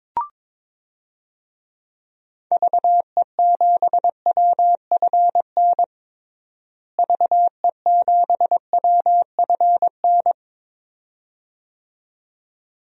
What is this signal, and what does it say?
Morse code, heard as a keyed tone, about 22 wpm, 700 Hz. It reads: VE7WFN VE7WFN
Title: DXCC-100 Call Signs - 6 Characters - Repeated 2x - 22wpm